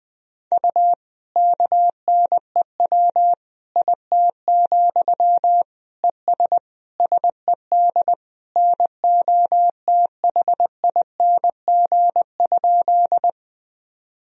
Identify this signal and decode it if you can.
Morse code, heard as a keyed tone, about 20 wpm, 700 Hz. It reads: U KNEW IT, ES SED NOTHING?
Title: Sentences from Top 500 Words and Abbreviations 20wpm